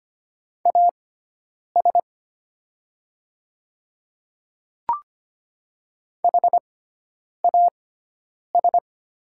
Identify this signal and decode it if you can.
Morse code, heard as a keyed tone, about 25 wpm, spaced as for 9 wpm, 690 Hz. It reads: AS HAS
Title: 6x Character Spacing - Top 200 Words - 25wpm